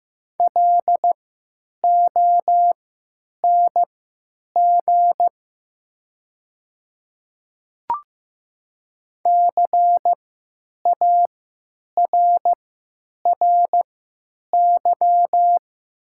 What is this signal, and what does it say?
Morse code, heard as a keyed tone, about 15 wpm, spaced as for 9 wpm, 690 Hz. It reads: LONG CARRY
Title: 3x Character Spacing - Top 300 Words - 15wpm